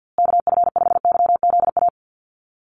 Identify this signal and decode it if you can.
Morse code, heard as a keyed tone, about 50 wpm, 700 Hz. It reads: KF5YZA
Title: USA Call Signs Repeated 3x 50wpm